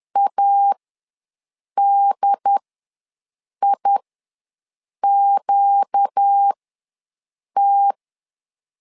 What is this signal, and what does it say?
Morse code, heard as a keyed tone, about 10 wpm, spaced as for 6 wpm, 800 Hz. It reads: ADIQT
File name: LEZIONI / 4 Q ws 10wpm 2306